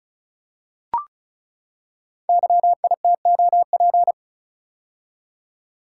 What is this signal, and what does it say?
Morse code, heard as a keyed tone, about 36 wpm, 690 Hz. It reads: YITOP
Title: Code Groups - 5 Letters - 35wpm